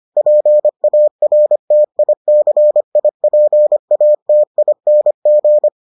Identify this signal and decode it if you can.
Morse code, heard as a keyed tone, about 25 wpm, 600 Hz. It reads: PARTICIPATING